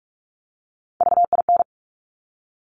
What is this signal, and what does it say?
Morse code, heard as a keyed tone, about 46 wpm, 700 Hz. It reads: VIN
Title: Top Three Letter Combinations in Words 45wpm